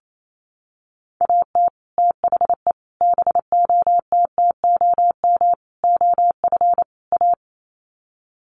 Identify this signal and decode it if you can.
Morse code, heard as a keyed tone, about 28 wpm, 700 Hz. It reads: AT THE BOTTOM OF A
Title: Sets of 5 Words Encore2 28wpm